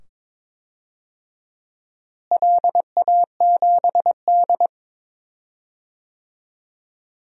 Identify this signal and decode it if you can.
Morse code, heard as a keyed tone, about 22 wpm, 700 Hz. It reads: LA7D
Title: DXCC-100 Call Signs - 4 Characters - Repeated 2x - 22wpm